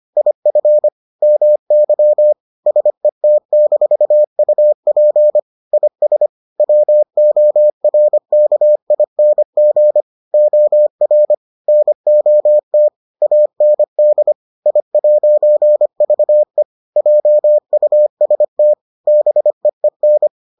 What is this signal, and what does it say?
Morse code, heard as a keyed tone, about 25 wpm, 600 Hz. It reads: IF MY SET-UP IS WORKING OR NOT AND I'VE JUST BEEN